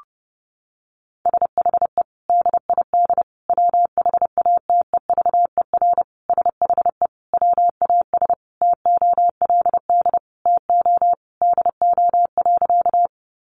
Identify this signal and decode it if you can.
Morse code, heard as a keyed tone, about 30 wpm, 700 Hz. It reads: SHE DID WHATEVER SHE WAS TOLD TO DO.
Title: Sentences from Top 1100 Words 30wpm